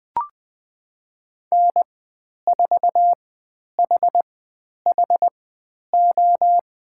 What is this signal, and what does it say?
Morse code, heard as a keyed tone, about 20 wpm, spaced as for 10 wpm, 700 Hz. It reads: N4HHO
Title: 095 Lesson 16g - Letter c - Call Signs 20wpm